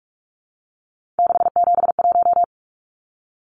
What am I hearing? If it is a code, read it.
Morse code, heard as a keyed tone, about 44 wpm, 700 Hz. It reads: 671
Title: Numbers - 3-Digit 45wpm